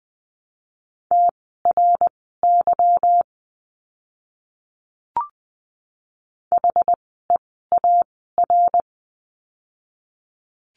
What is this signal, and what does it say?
Morse code, heard as a keyed tone, about 19 wpm, spaced as for 14 wpm, 700 Hz. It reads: TRY HEAR
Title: 206 Lesson 33f - Letter j - Words 20wpm